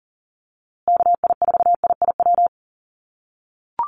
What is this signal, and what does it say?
Morse code, heard as a keyed tone, about 40 wpm, 700 Hz. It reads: KI4IIW
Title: USA Call Signs Repeated 3x 40wpm